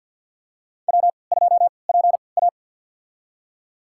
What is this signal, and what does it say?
Morse code, heard as a keyed tone, about 53 wpm, spaced as for 29 wpm, 700 Hz. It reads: W2PA
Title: USA Call Signs - 4 Characters - 3x Character Spacing - 50wpm